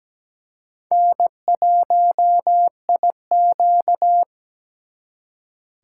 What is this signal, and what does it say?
Morse code, heard as a keyed tone, about 20 wpm, 700 Hz. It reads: N1IQ
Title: Speed Racing - USA Call Signs 17wpm